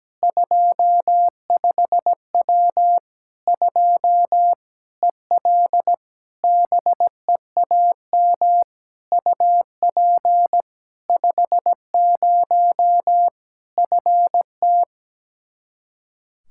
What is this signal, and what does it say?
Morse code, heard as a keyed tone, about 17 wpm, 700 Hz. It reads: 25W 2 EL BEAM UP 50 FT
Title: Practice QSO - Exchange 2 - Info - 17wpm